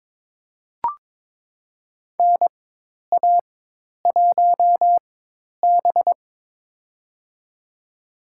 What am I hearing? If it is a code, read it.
Morse code, heard as a keyed tone, about 22 wpm, spaced as for 10 wpm, 700 Hz. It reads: NA1B
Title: USA Call Signs - 4 Characters - 4x Character Spacing - 22wpm